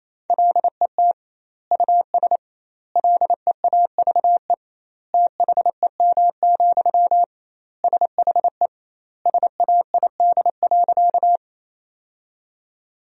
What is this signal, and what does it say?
Morse code, heard as a keyed tone, about 28 wpm, 700 Hz. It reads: LET US LEAVE THEM, SHE SAID.
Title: 2x Word-Spacing - Sentences from Top 400 Words - 28wpm